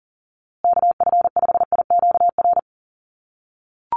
Morse code, heard as a keyed tone, about 40 wpm, 700 Hz. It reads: KF5IQR
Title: USA Call Signs Repeated 3x 40wpm